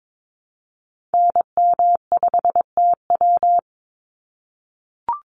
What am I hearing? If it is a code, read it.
Morse code, heard as a keyed tone, about 22 wpm, 700 Hz. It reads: NM5TW
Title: NA SOTA Activator Callsigns 22wpm